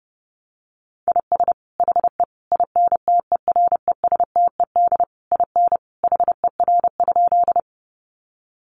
Morse code, heard as a keyed tone, about 30 wpm, 700 Hz. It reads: IS HE INTERESTED IN HER?
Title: Sentences from Top 900 Words 30wpm